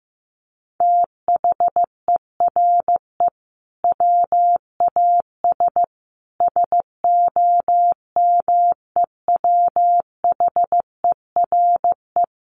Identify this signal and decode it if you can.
Morse code, heard as a keyed tone, about 15 wpm, 700 Hz. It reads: THERE WAS SOMEWHERE
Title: Sentences from Top 1400 Words 15wpm